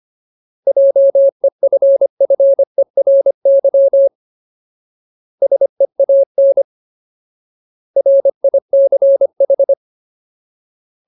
Morse code, heard as a keyed tone, about 25 wpm, 550 Hz. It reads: JEFFERY SEAN RICH
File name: Ultimate Word List / 25WPM Ultimate